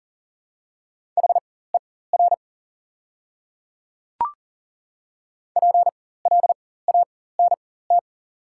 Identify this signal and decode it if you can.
Morse code, heard as a keyed tone, about 40 wpm, spaced as for 19 wpm, 700 Hz. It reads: HER PLANT
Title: 4x Character Spacing - Top 500 Words - 40wpm